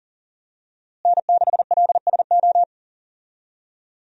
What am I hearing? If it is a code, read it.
Morse code, heard as a keyed tone, about 37 wpm, 700 Hz. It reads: N6LSO
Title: USA Call Signs - 5 Characters - Repeated 2x - 40wpm